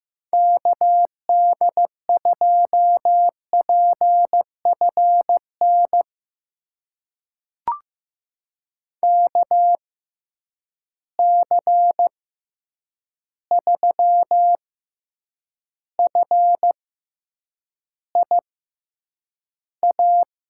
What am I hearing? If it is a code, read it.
Morse code, heard as a keyed tone, about 15 wpm, spaced as for 8 wpm, 700 Hz. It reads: KD2PFN KC3FIA
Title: USA Call Signs - 6 Characters - 6x Character Spacing - 15wpm